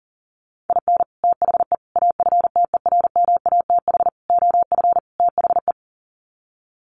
Morse code, heard as a keyed tone, about 39 wpm, 700 Hz. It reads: IN THE AFTERMATH OF THE
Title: Sets of 5 Words 40wpm